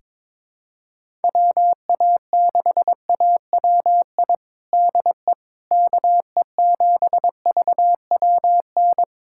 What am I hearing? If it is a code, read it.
Morse code, heard as a keyed tone, about 22 wpm, 700 Hz. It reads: WA6AWI DE KE7VWN
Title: USA Callsign DE Callsign - 6 Characters - 22wpm